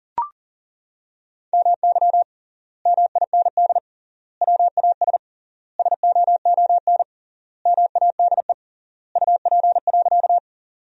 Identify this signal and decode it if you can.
Morse code, heard as a keyed tone, about 40 wpm, 700 Hz. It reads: MY MIND WAS SOON MADE UP.
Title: 3x Word-Spacing - Sentences from Top 500 Words - 40wpm